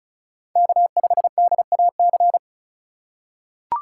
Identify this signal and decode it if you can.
Morse code, heard as a keyed tone, about 36 wpm, 700 Hz. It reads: K5DAC
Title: USA Call Signs - 5 Characters - 2x Character Spacing - 35wpm